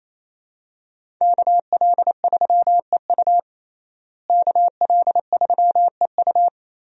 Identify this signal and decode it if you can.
Morse code, heard as a keyed tone, about 28 wpm, 700 Hz. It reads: KL3EU KL3EU
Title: USA Call Signs - 5 Characters - Repeated 3x - 28wpm